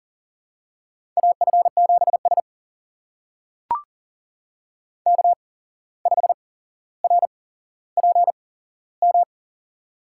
Morse code, heard as a keyed tone, about 40 wpm, spaced as for 16 wpm, 700 Hz. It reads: AF7S K5RPM
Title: USA Call Signs - 8x Character Spacing - 40wpm